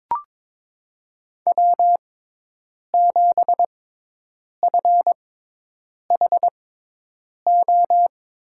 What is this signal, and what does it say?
Morse code, heard as a keyed tone, about 22 wpm, spaced as for 8 wpm, 700 Hz. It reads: W7FHO
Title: USA Call Signs - 5 Characters - 6x Character Spacing - 22wpm